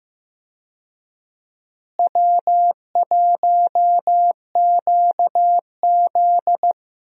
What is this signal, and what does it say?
Morse code, heard as a keyed tone, about 15 wpm, 700 Hz. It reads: W1QZ
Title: USA Call Signs - 4 Characters - Repeated 2x - 15wpm